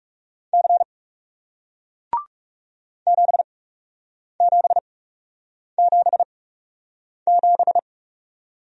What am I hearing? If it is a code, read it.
Morse code, heard as a keyed tone, about 40 wpm, spaced as for 8 wpm, 700 Hz. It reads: C 7777